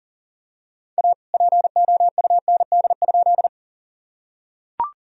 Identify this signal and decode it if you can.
Morse code, heard as a keyed tone, about 40 wpm, 700 Hz. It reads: A POUND?
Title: Sentences from Top 1000 Words - Review Part 2 - 40wpm